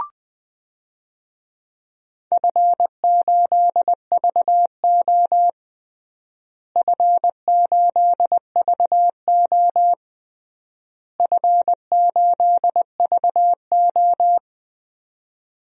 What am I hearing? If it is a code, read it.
Morse code, heard as a keyed tone, about 20 wpm, 700 Hz. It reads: F8VO F8VO F8VO
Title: DXCC-100 Call Signs - 4 Characters - Repeated 3x - 20wpm